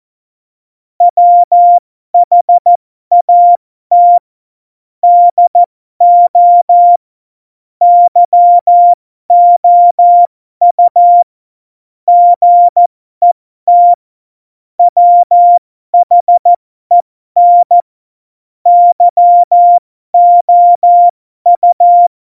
Morse code, heard as a keyed tone, about 14 wpm, spaced as for 12 wpm, 700 Hz. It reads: WHAT DO YOU GET WHEN YOU
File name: Morsekurs / Witze